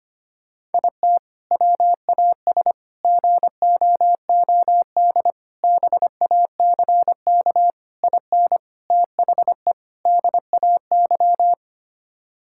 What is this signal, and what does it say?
Morse code, heard as a keyed tone, about 25 wpm, 700 Hz. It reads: IT WAS GOOD BACK IN THE DAY
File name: practice / 25wpm / Sentences with 3-Word Phrases - 25wpm